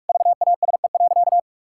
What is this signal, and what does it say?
Morse code, heard as a keyed tone, about 46 wpm, 700 Hz. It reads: VASE.
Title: Pangram Challenge v2 - 45wpm